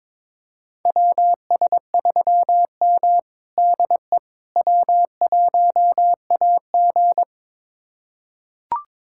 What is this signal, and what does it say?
Morse code, heard as a keyed tone, about 22 wpm, 700 Hz. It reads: WS3M DE W1AG
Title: USA Callsign DE Callsign - 4 Characters - 22wpm